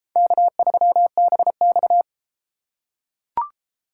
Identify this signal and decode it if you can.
Morse code, heard as a keyed tone, about 30 wpm, 700 Hz. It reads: K3BX